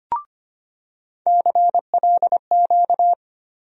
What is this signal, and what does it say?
Morse code, heard as a keyed tone, about 25 wpm, 700 Hz. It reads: CLQ
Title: USA Call Sign Postfixes - 3 Letters 25wpm